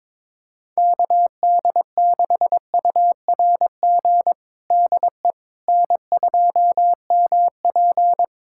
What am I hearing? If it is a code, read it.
Morse code, heard as a keyed tone, about 22 wpm, 700 Hz. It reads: KD6URG DE N2MP
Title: USA Callsign DE Callsign 22wpm